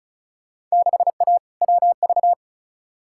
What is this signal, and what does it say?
Morse code, heard as a keyed tone, about 34 wpm, 700 Hz. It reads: 6A WV
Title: Field Day Exchanges - 35wpm